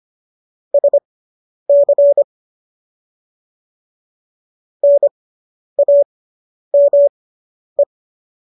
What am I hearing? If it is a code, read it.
Morse code, heard as a keyed tone, about 25 wpm, spaced as for 10 wpm, 570 Hz. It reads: SC NAME